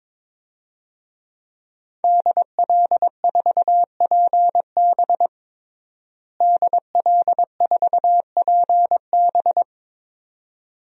Morse code, heard as a keyed tone, about 22 wpm, 700 Hz. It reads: DL4PB DL4PB
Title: DXCC-100 Call Signs - 5 Characters - Repeated 2x - 22wpm